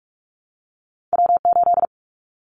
Morse code, heard as a keyed tone, about 46 wpm, 700 Hz. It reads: W8
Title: Callsign Prefixes for Top 25 DXCC - 45wpm